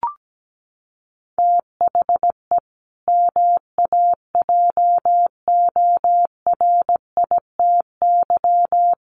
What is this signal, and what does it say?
Morse code, heard as a keyed tone, about 17 wpm, 700 Hz. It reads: THE MAJORITY
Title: Sets of 4 Words 17wpm